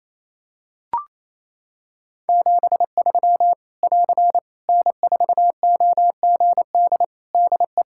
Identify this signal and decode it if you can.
Morse code, heard as a keyed tone, about 27 wpm, 700 Hz. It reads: 73 <AR> N4OGD DE
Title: K1USN SST QSO Practice 28wpm